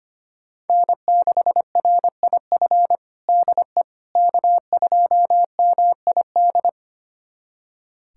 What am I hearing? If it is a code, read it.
Morse code, heard as a keyed tone, about 25 wpm, 700 Hz. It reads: N6RIF DE K2MID
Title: USA Callsign DE Callsign - 5 Characters - 25wpm